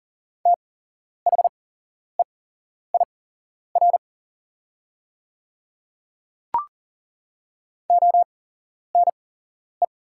Morse code, heard as a keyed tone, about 40 wpm, spaced as for 11 wpm, 700 Hz. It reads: THEIR ONE
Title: 8x Character Spacing - Top 100 Words - 40wpm